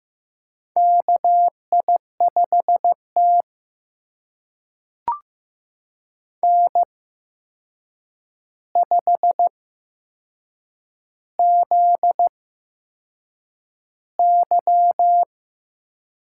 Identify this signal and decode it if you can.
Morse code, heard as a keyed tone, about 15 wpm, spaced as for 6 wpm, 700 Hz. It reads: KI5T N5ZY
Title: USA Call Signs - 4 Characters - 8x Character Spacing - 15wpm